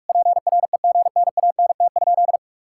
Morse code, heard as a keyed tone, about 46 wpm, 700 Hz. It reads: PREGNANT?
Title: Sentences from Top 1000 Words - Review Part 6 - 45wpm